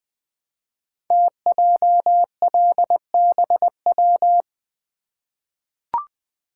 Morse code, heard as a keyed tone, about 20 wpm, 700 Hz. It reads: TJLBW
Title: Code Groups - 5 Letter:Numbers - 20wpm